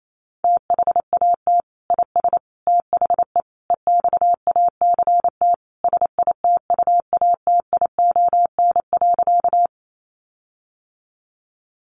Morse code, heard as a keyed tone, about 28 wpm, 700 Hz. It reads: THAT IS THE EXACT SITUATION.
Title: Sentences from Top 1300 Words 28wpm